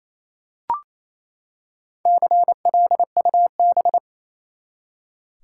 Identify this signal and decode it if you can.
Morse code, heard as a keyed tone, about 28 wpm, 700 Hz. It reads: CLUB
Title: Four Letter Words - 28wpm